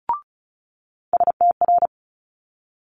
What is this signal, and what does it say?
Morse code, heard as a keyed tone, about 36 wpm, 700 Hz. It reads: STR
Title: Multi-letter Phonemes Quick Fire 35wpm